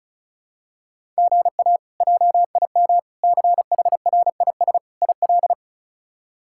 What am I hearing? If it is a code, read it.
Morse code, heard as a keyed tone, about 36 wpm, 700 Hz. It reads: GA JIM CHRIS IL
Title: K1USN SST QSO Practice - 35wpm